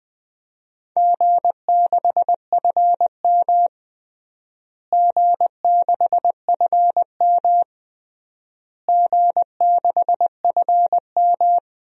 Morse code, heard as a keyed tone, about 21 wpm, 700 Hz. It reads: G6FM G6FM G6FM
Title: DXCC-100 Call Signs - 4 Characters - Repeated 3x - 20wpm